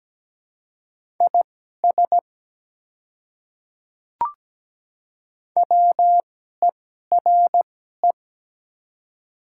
Morse code, heard as a keyed tone, about 17 wpm, spaced as for 12 wpm, 700 Hz. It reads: IS WERE
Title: 2x Character Spacing - Top 100 Words - 17wpm